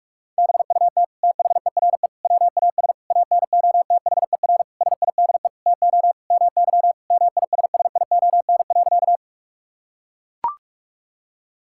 Morse code, heard as a keyed tone, about 42 wpm, 700 Hz. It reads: BUT THERE WAS ANOTHER SIDE TO MY MISSION.
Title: Sentences from Top 1600 Words 45wpm